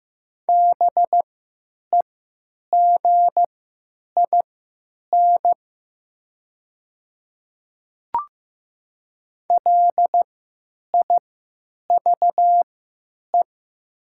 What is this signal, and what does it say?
Morse code, heard as a keyed tone, about 15 wpm, spaced as for 9 wpm, 700 Hz. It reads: BEGIN LIVE
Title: 3x Character Spacing - Top 300 Words - 15wpm